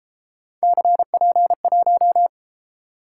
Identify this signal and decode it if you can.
Morse code, heard as a keyed tone, about 31 wpm, 700 Hz. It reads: CP1